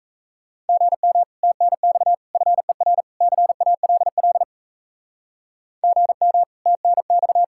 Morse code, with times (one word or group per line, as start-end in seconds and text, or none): 0.7-1.2 GM
1.4-2.1 TNX
2.4-3.0 FER
3.2-4.4 CALL
5.8-6.4 GM
6.7-7.4 TNX